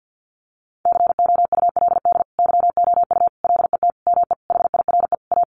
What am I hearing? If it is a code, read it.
Morse code, heard as a keyed tone, about 50 wpm, 700 Hz. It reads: COULD YOU LET ME HIRE A
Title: Sentences from Top 1400 Words 50wpm